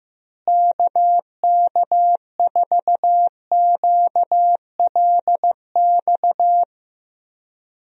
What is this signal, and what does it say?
Morse code, heard as a keyed tone, about 15 wpm, 700 Hz. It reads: KK4QLX
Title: USA Call Signs - 6 Characters - Repeated 3x - 15wpm